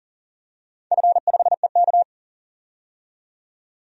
Morse code, heard as a keyed tone, about 38 wpm, 700 Hz. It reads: F5EK
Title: DXCC-100 Call Signs - 4 Characters - 40wpm